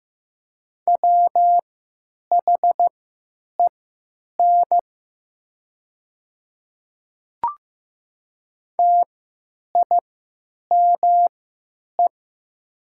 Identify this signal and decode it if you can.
Morse code, heard as a keyed tone, about 15 wpm, spaced as for 9 wpm, 700 Hz. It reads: WHEN TIME
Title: 3x Character Spacing - Top 100 Words - 15wpm